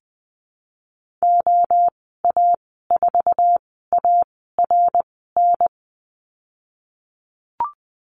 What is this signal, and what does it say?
Morse code, heard as a keyed tone, about 20 wpm, 700 Hz. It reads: O A 4 A R N
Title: 055 Lesson 10g - Letter r - Call Signs 20wpm